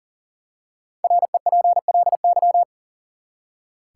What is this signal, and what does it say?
Morse code, heard as a keyed tone, about 39 wpm, 700 Hz. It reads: REPLY
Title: Five Letter Words - 40wpm